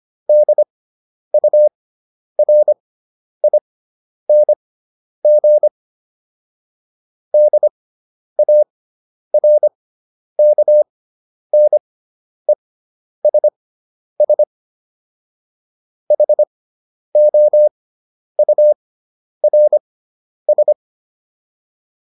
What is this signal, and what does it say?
Morse code, heard as a keyed tone, about 25 wpm, spaced as for 10 wpm, 600 Hz. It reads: DURING DARKNESS HOURS